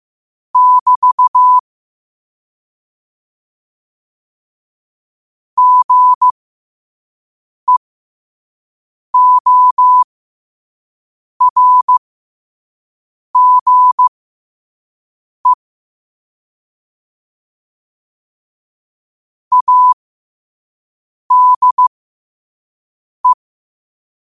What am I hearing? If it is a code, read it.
Morse code, heard as a keyed tone, about 15 wpm, spaced as for 5 wpm, 1000 Hz. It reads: <BT> GEORGE ADE
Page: Quotes for Sat, 16 Aug 2025 in Morse Code at 5 words per minute.